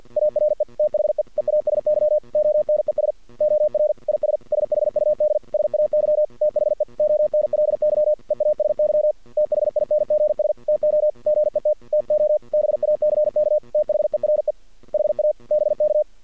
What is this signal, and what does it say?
Morse code, heard as a keyed tone, about 50 wpm, 610 Hz. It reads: ARE THE EASY ONES OR IS SHUD SAY THE ONLY EASY THINGS TO GET TO BACK THERE HA BK
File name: archived / CRSnet